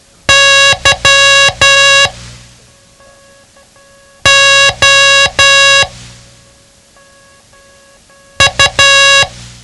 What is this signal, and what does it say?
Morse code, heard as a keyed tone, about 12 wpm, spaced as for 3 wpm, 1650 Hz. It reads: YOU
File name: ham radio / CodeQuick / Code Quick 07